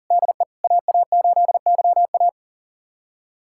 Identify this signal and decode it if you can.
Morse code, heard as a keyed tone, about 38 wpm, 700 Hz. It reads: DE AA8YA
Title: USA Callsign DE Callsign - 5 Characters - 40wpm